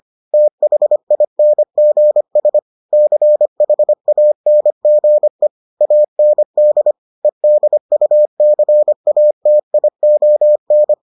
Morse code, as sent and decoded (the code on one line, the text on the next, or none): - .... .. -. --. ... / -.-. .... .- -. --. / .- -. -.. / . -.. ..- -.-. .- - .. --- -.
THINGS CHANGE AND EDUCATION